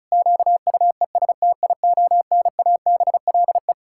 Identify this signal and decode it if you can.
Morse code, heard as a keyed tone, about 36 wpm, 700 Hz. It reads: QUESTIONABLE
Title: Pangram Challenge v2 - 35wpm